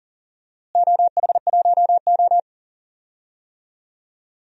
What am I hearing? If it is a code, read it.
Morse code, heard as a keyed tone, about 39 wpm, 700 Hz. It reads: OH1O
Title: DXCC-100 Call Signs - 4 Characters - Repeated 2x - 40wpm